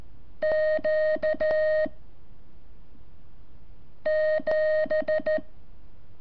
Morse code, heard as a keyed tone, about 14 wpm, 620 Hz. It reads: Q 7
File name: curso / ejercicio 28